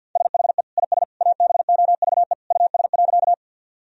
Morse code, heard as a keyed tone, about 50 wpm, 700 Hz. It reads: SHE IS ABOVE US.
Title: Sentences from Top 1000 Words - Review Part 2 - 50wpm